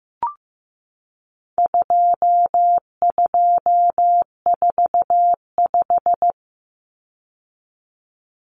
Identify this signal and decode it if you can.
Morse code, heard as a keyed tone, about 15 wpm, 700 Hz. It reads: 2245
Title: Numbers - 4-Digit 15wpm